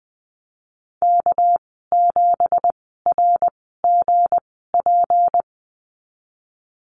Morse code, heard as a keyed tone, about 20 wpm, 700 Hz. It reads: K 7 R G P
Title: 207 Lesson 33g - Letter j - Call Signs 20wpm